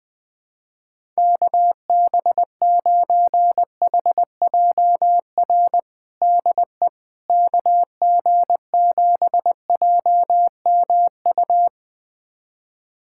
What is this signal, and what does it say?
Morse code, heard as a keyed tone, about 20 wpm, 700 Hz. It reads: KB9HJR DE KG7JMU
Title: USA Callsign DE Callsign - 6 Characters - 20wpm